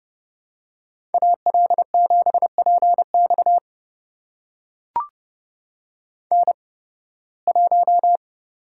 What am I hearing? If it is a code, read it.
Morse code, heard as a keyed tone, about 31 wpm, spaced as for 20 wpm, 700 Hz. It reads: AL7PX N1
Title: USA Call Signs - 5 Characters - 8x Character Spacing - 30wpm